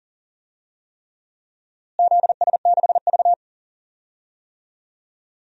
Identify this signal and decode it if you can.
Morse code, heard as a keyed tone, about 40 wpm, 700 Hz. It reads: ZS6V
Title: DXCC-100 Call Signs - 4 Characters - Repeated 3x - 40wpm